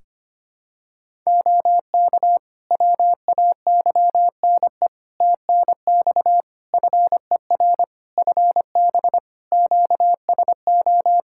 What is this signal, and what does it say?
Morse code, heard as a keyed tone, about 25 wpm, 700 Hz. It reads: OK WAYNE TNX FER FB QSO
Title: Practice QSO - Exchange 4 - Ending - 25wpm